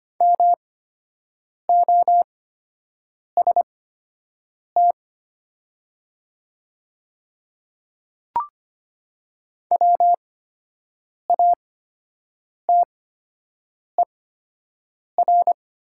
Morse code, heard as a keyed tone, about 25 wpm, spaced as for 7 wpm, 700 Hz. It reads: MOST WATER
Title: 8x Character Spacing - Top 500 Words - 25wpm